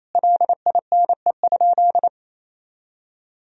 Morse code, heard as a keyed tone, about 28 wpm, 700 Hz. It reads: LINE?
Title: Sentences from Top 1000 Words - Review Part 1 - 28wpm